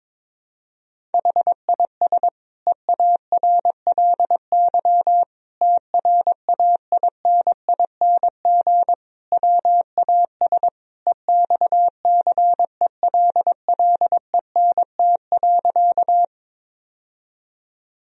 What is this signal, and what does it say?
Morse code, heard as a keyed tone, about 22 wpm, 700 Hz. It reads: HIS EARLY TRAINING WAS EXCELLENT.
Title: Sentences from Top 1000 Words - Review Part 4 - 22wpm